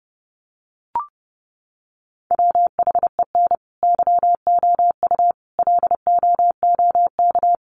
Morse code, heard as a keyed tone, about 30 wpm, 700 Hz. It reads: WHEN YOU LOOK
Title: Sets of 5 Words 30wpm